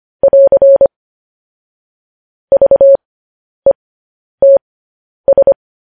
Morse code, heard as a keyed tone, about 25 wpm, spaced as for 10 wpm, 550 Hz. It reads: <AR> VETS
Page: News Headlines in Morse Code at 10 WPM